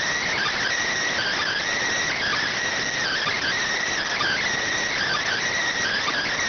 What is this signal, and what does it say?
Emergency Alert System (SAME) tones (header bursts), heard as harsh buzzing bursts. It is heard from 1.1 to 5.4 s.